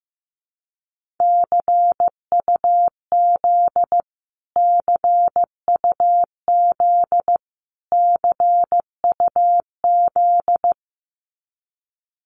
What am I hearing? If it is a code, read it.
Morse code, heard as a keyed tone, about 15 wpm, 700 Hz. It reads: CUZ CUZ CUZ
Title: Abbreviations Repeated 3x - 15wpm